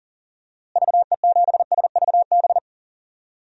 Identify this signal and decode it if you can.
Morse code, heard as a keyed tone, about 40 wpm, 700 Hz. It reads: VE7SVB
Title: DXCC-100 Call Signs - 6 Characters - Repeated 2x - 40wpm